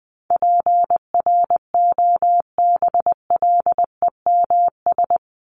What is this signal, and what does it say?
Morse code, heard as a keyed tone, about 20 wpm, 700 Hz. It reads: PROBLEMS
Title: ES:S Suffix - Top 100 Words with Base Words 20wpm